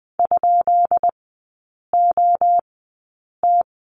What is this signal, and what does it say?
Morse code, heard as a keyed tone, about 20 wpm, spaced as for 8 wpm, 700 Hz. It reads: ?OT
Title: 145 Lesson 24e - Letter y - Three Characters 20wpm